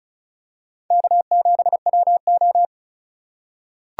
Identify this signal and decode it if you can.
Morse code, heard as a keyed tone, about 34 wpm, 700 Hz. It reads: K7WO
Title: USA Call Signs - 4 Characters - 2x Character Spacing - 35wpm